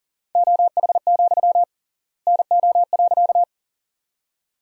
Morse code, heard as a keyed tone, about 38 wpm, 700 Hz. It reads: OH, NO.
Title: 3x Word-Spacing - Sentences from Top 500 Words - 40wpm